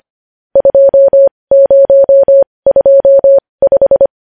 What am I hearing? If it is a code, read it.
Morse code, heard as a keyed tone, about 25 wpm, spaced as for 20 wpm, 550 Hz. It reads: 2025
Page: News Headlines in Morse Code at 20 WPM